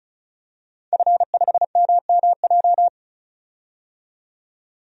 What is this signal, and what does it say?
Morse code, heard as a keyed tone, about 36 wpm, 700 Hz. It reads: F5MMJ
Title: DXCC-100 Call Signs - 5 Characters - 35wpm